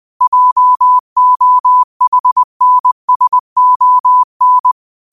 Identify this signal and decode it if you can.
Morse code, heard as a keyed tone, about 20 wpm, 1000 Hz. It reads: JOHNSON
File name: qotdfiles / 20wpm